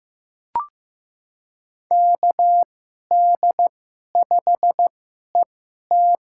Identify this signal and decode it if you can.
Morse code, heard as a keyed tone, about 15 wpm, spaced as for 11 wpm, 700 Hz. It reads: KD5ET
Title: USA Call Signs - 2x Character Spacing - 15wpm